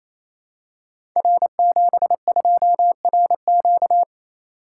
Morse code, heard as a keyed tone, about 27 wpm, 700 Hz. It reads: R72RQ
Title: Code Groups - 5 Letter:Numbers - 28wpm